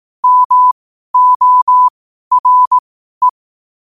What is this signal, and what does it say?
Morse code, heard as a keyed tone, about 18 wpm, spaced as for 12 wpm, 1000 Hz. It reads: MORE
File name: qotdfiles / 12wpm